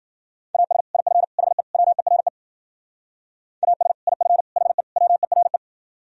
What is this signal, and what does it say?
Morse code, heard as a keyed tone, about 50 wpm, 700 Hz. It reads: AS IF HE WERE AS IF HE WERE